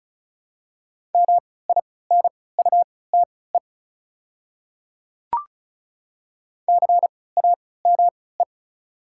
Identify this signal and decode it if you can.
Morse code, heard as a keyed tone, about 36 wpm, spaced as for 20 wpm, 690 Hz. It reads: MINUTE CAME
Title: 3x Character Spacing - Top 500 Words - 35wpm